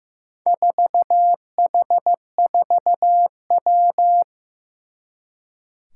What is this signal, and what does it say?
Morse code, heard as a keyed tone, about 15 wpm, 700 Hz. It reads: 4H4W
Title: Four Challenging Letter-Number Combinations 15wpm